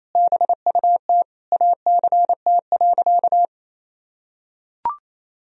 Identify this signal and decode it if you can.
Morse code, heard as a keyed tone, about 27 wpm, 700 Hz. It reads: BUT ACT.
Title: Sentences from Top 1000 Words - Review Part 1 - 28wpm